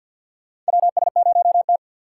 Morse code, heard as a keyed tone, about 50 wpm, 700 Hz. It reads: WS0T
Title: USA Call Signs - 2x Character Spacing - 50wpm